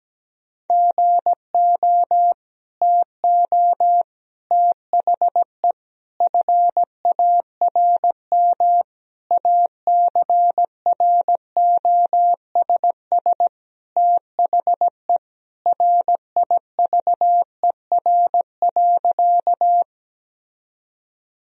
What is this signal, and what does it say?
Morse code, heard as a keyed tone, about 17 wpm, 700 Hz. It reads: GO TO THE FARM ACROSS THE RIVER.
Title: Sentences from Top 1000 Words - Review Part 2 - 17wpm